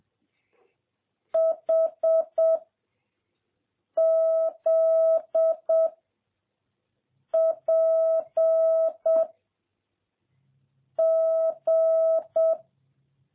Morse code, heard as a keyed tone, about 7 wpm, 650 Hz. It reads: H Z P G
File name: jh8cbh / 2021cw